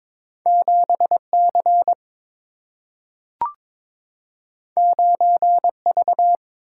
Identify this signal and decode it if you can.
Morse code, heard as a keyed tone, about 22 wpm, 700 Hz. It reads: 7C 9V
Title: Single-Letter Single-Number 22wpm